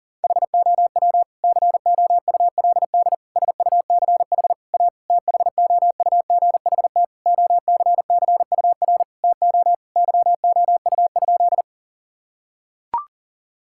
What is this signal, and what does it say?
Morse code, heard as a keyed tone, about 40 wpm, 700 Hz. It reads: HOW COULD SUCH A THOUGHT OCCUR TO YOU?
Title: Sentences from Top 1000 Words - Review Part 6 - 40wpm